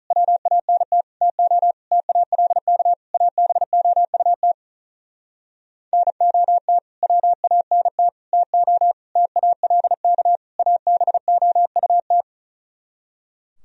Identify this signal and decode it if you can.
Morse code, heard as a keyed tone, about 46 wpm, 700 Hz. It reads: WANT TO TALK ABOUT NOT WANT TO TALK ABOUT